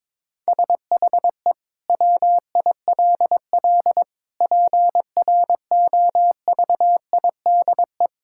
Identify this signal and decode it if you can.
Morse code, heard as a keyed tone, about 22 wpm, 700 Hz. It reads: SHE WILL PROVIDE
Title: Sentences from Top 1000 Words - Review Part 6 - 22wpm